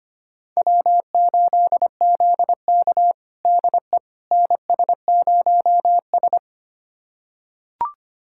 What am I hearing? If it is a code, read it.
Morse code, heard as a keyed tone, about 25 wpm, 700 Hz. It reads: W8ZK DE NS0S
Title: USA Callsign DE Callsign - 4 Characters - 25wpm